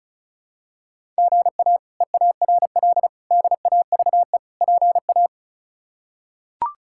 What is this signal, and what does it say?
Morse code, heard as a keyed tone, about 36 wpm, 700 Hz. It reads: GA EARL DAVE PA
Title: K1USN SST QSO Practice - 35wpm